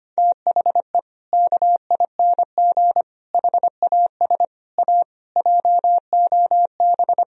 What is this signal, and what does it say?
Morse code, heard as a keyed tone, about 25 wpm, 700 Hz. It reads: THE KING HAS A JOB
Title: Sentences from Top 1000 Words - Review Part 3 - 25wpm